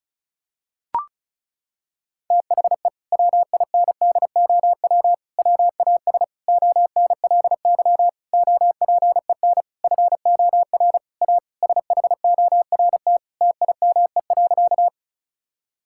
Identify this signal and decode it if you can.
Morse code, heard as a keyed tone, about 36 wpm, 700 Hz. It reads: THE WINDOW WAS ONLY OPEN FOR A SHORT TIME.
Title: Sentences from Top 1600 Words 35wpm